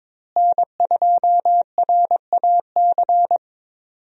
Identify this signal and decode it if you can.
Morse code, heard as a keyed tone, about 22 wpm, 700 Hz. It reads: N2RAC
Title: USA Call Signs - 5 Characters - Repeated 3x - 22wpm